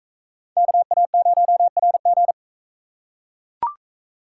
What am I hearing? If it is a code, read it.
Morse code, heard as a keyed tone, about 42 wpm, 700 Hz. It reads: KA0RG